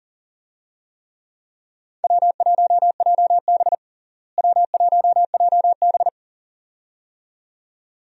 Morse code, heard as a keyed tone, about 40 wpm, 700 Hz. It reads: W1JB W1JB
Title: USA Call Signs - 4 Characters - Repeated 2x - 40wpm